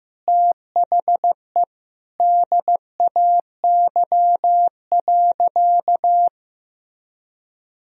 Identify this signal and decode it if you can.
Morse code, heard as a keyed tone, about 15 wpm, 700 Hz. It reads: THE DAY.
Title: Sentences from Top 1000 Words - Review Part 3 - 15wpm